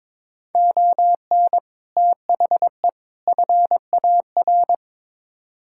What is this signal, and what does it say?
Morse code, heard as a keyed tone, about 22 wpm, 700 Hz. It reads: ON THE FAR
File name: practice / 22wpm / Build A Phrase - 5-Word Phrases - 22wpm